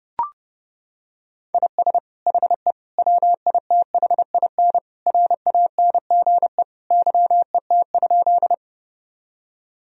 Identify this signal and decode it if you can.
Morse code, heard as a keyed tone, about 30 wpm, 700 Hz. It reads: IS HE WITHIN RANGE YET?
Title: Sentences from Top 1000 Words - Review Part 4 - 30wpm